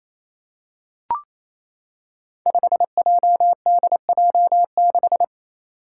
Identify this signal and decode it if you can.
Morse code, heard as a keyed tone, about 29 wpm, 700 Hz. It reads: 5JDJ6
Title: Five Challenging Letter-Number Combinations 28wpm